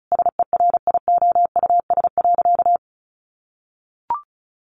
Morse code, heard as a keyed tone, about 34 wpm, 700 Hz. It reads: SERIOUS.
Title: Sentences from Top 1300 Words 35wpm